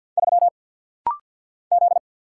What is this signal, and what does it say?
Morse code, heard as a keyed tone, about 50 wpm, 700 Hz. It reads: 3 Z
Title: Single Letter-Number Alt - Rapid-Fire 50wpm